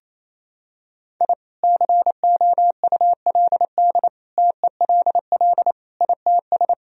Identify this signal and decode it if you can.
Morse code, heard as a keyed tone, about 28 wpm, 700 Hz. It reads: I COULD TELL ITS
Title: Sentences from Top 1000 Words - Review Part 2 - 28wpm